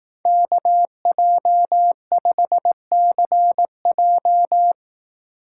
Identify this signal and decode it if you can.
Morse code, heard as a keyed tone, about 17 wpm, 700 Hz. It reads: KJ5CJ
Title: Speed Racing - USA Call Signs 15wpm